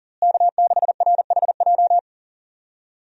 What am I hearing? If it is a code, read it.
Morse code, heard as a keyed tone, about 37 wpm, 700 Hz. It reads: K6RHJ
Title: USA Call Signs - 5 Characters - Repeated 2x - 40wpm